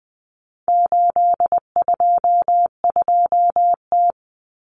Morse code, heard as a keyed tone, about 20 wpm, 700 Hz. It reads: 822T
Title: Common Cut Numbers - 4-Digit 20wpm